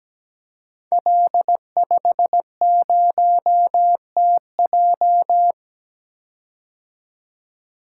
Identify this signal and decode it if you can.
Morse code, heard as a keyed tone, about 17 wpm, 700 Hz. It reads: L50TJ
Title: Code Groups - 5 Letter:Numbers - 17wpm